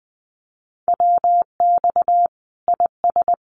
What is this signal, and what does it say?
Morse code, heard as a keyed tone, about 20 wpm, 700 Hz. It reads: WX IS
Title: Common QSO Phrases 20wpm